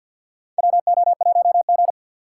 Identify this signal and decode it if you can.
Morse code, heard as a keyed tone, about 50 wpm, 700 Hz. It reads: WO1G
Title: USA Call Signs - 4 Characters - 6x Character Spacing - 50wpm